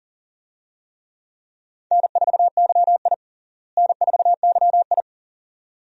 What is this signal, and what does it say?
Morse code, heard as a keyed tone, about 40 wpm, 700 Hz. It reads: N4YI N4YI
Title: USA Call Signs - 4 Characters - Repeated 2x - 40wpm